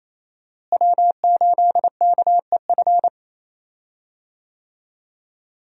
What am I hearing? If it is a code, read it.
Morse code, heard as a keyed tone, about 28 wpm, 700 Hz. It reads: W8KEF
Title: USA Call Signs - 5 Characters - Repeated 2x - 28wpm